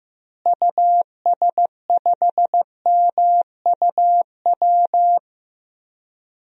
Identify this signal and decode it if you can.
Morse code, heard as a keyed tone, about 15 wpm, 700 Hz. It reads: US5MUW
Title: DXCC-100 Call Signs - 6 Characters - Repeated 3x - 15wpm